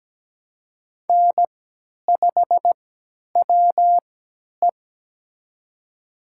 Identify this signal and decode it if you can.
Morse code, heard as a keyed tone, about 17 wpm, spaced as for 10 wpm, 700 Hz. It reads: N5WE
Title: USA Call Signs - 4 Characters - 3x Character Spacing - 17wpm